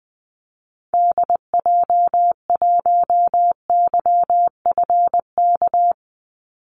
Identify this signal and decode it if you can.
Morse code, heard as a keyed tone, about 20 wpm, 700 Hz. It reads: DJ1YFK
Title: Call Signs Repeated 2x - 20wpm